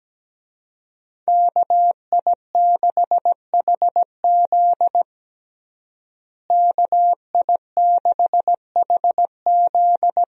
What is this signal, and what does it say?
Morse code, heard as a keyed tone, about 17 wpm, 700 Hz. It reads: KI6HZ KI6HZ
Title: USA Call Signs - 5 Characters - Repeated 3x - 17wpm